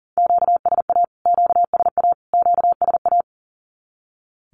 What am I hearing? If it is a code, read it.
Morse code, heard as a keyed tone, about 40 wpm, 700 Hz. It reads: QSA QSA QSA
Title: Q-Codes Repeated 3x 40wpm